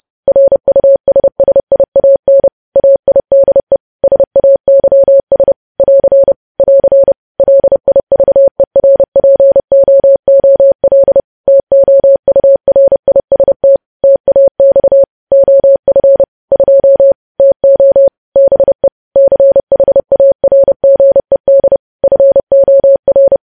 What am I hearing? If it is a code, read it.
Morse code, heard as a keyed tone, about 30 wpm, 550 Hz. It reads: RUSSIAN AIDE SAYS <AR> <AR> LIVERPOOL TOURIST TAX OF 2 TO BE CHARGED FOR
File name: podcast / News headlines-1x-30x30